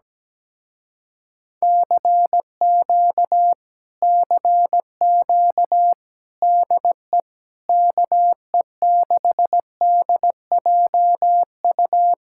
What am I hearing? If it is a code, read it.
Morse code, heard as a keyed tone, about 17 wpm, 700 Hz. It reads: CQ CQ DE KE6DJU
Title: Speed-Racing Common QSO Phrases Encore - 17wpm